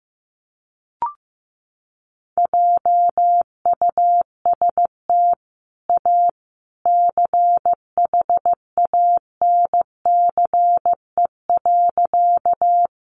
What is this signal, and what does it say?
Morse code, heard as a keyed tone, about 15 wpm, 700 Hz. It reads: JUST A CHANCE.
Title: Sentences from Top 800 Words 15wpm